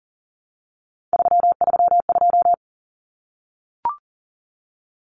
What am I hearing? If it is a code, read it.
Morse code, heard as a keyed tone, about 40 wpm, 700 Hz. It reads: 332
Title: Numbers - 3-Digit 40wpm